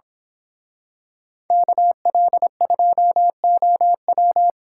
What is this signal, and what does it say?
Morse code, heard as a keyed tone, about 26 wpm, 700 Hz. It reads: KL2OW
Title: Speed Racing - USA Call Signs 22wpm